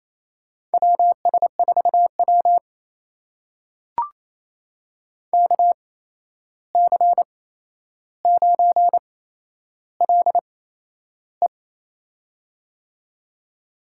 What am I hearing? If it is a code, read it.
Morse code, heard as a keyed tone, about 28 wpm, spaced as for 11 wpm, 700 Hz. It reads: WS4W KC9LE
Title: USA Call Signs - 8x Character Spacing - 28wpm